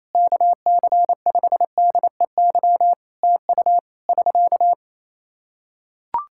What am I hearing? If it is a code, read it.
Morse code, heard as a keyed tone, about 28 wpm, 700 Hz. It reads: KC5DEY TU <SK>
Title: Practice QSO - Exchange 4 - Ending - 28wpm